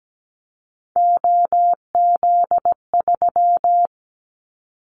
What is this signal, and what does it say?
Morse code, heard as a keyed tone, about 17 wpm, 700 Hz. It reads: OZ3
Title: Callsign Prefixes for Top 100 DXCC - 17wpm